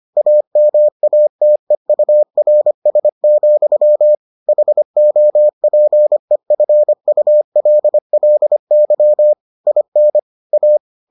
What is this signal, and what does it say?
Morse code, heard as a keyed tone, about 25 wpm, 600 Hz. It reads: AMATEURS, HOPEFULLY IN A